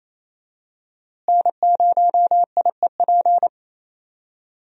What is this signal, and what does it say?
Morse code, heard as a keyed tone, about 28 wpm, 700 Hz. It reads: N0IEP
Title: USA Call Signs - 5 Characters - Repeated 3x - 28wpm